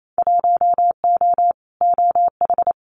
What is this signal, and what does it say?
Morse code, heard as a keyed tone, about 28 wpm, 700 Hz. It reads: 1O OH
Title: Winter Field Day Exchanges 28wpm